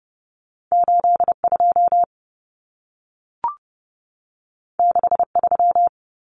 Morse code, heard as a keyed tone, about 30 wpm, 700 Hz. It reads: 82 63
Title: Common Cut Numbers - 2-Digit 30wpm